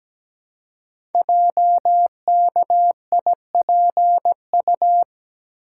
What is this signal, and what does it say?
Morse code, heard as a keyed tone, about 17 wpm, 700 Hz. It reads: JKIPU
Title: Code Groups - 5 Letters - 17wpm